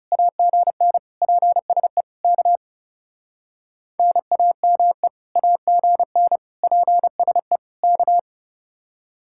Morse code, heard as a keyed tone, about 38 wpm, 700 Hz. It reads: AGN PSE K NAME AGN PSE K